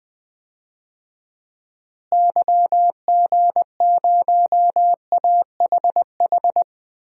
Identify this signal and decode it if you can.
Morse code, heard as a keyed tone, about 21 wpm, 700 Hz. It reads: YG0AHH
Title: DXCC-100 Call Signs - 6 Characters - Repeated 3x - 20wpm